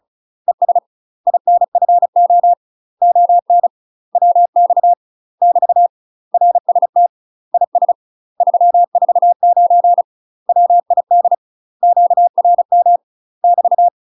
Morse code, heard as a keyed tone, about 36 wpm, 700 Hz. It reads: ES INFO ON WX <BT> RST IS 349 WID QRM <BT>
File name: UK MASTER QSOs TRAINING / 35WPM / UK MASTER QSOs 0031